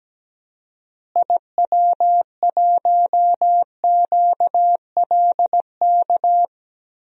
Morse code, heard as a keyed tone, about 17 wpm, 700 Hz. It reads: IW1QLK
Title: DXCC-100 Call Signs - 6 Characters - Repeated 2x - 17wpm